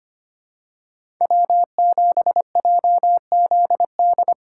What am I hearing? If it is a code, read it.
Morse code, heard as a keyed tone, about 25 wpm, 700 Hz. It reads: W7JZD
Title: USA Call Signs - 5 Characters - 4x Character Spacing - 25wpm